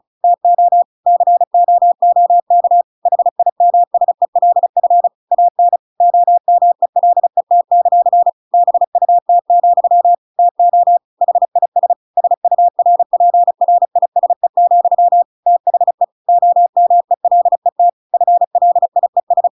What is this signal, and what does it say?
Morse code, heard as a keyed tone, about 34 wpm, 700 Hz. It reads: TO COOK HIMSELF AN OMELET; BUT, TO HIS SURPRISE, THE OMELET FLIES
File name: Pinocchio / Pinocchio ch5 35 35 ik1hge